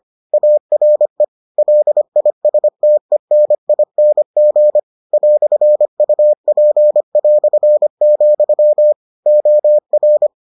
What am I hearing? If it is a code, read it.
Morse code, heard as a keyed tone, about 25 wpm, 600 Hz. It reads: ARE LISTENING "UP", OR